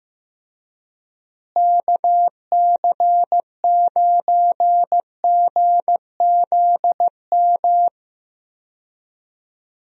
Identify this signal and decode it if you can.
Morse code, heard as a keyed tone, about 15 wpm, 700 Hz. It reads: KC9GZM
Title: USA Call Signs - 6 Characters - Repeated 3x - 15wpm